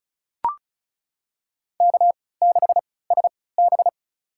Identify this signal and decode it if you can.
Morse code, heard as a keyed tone, about 36 wpm, spaced as for 20 wpm, 700 Hz. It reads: K6SB
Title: USA Call Signs - 3x Character Spacing - 35wpm